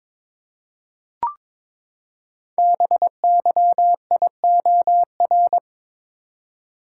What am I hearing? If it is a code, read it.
Morse code, heard as a keyed tone, about 22 wpm, 700 Hz. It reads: BYIOR
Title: Code Groups - 5 Letters - 22wpm